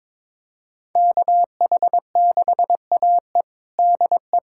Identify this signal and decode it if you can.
Morse code, heard as a keyed tone, about 22 wpm, 700 Hz. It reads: KH6AE DE
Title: USA Callsign DE Callsign - 5 Characters - 22wpm